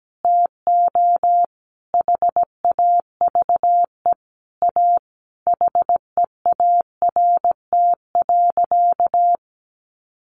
Morse code, heard as a keyed tone, about 17 wpm, 700 Hz. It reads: TO HAVE A HEART.
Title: Sentences from Top 1500 Words 17wpm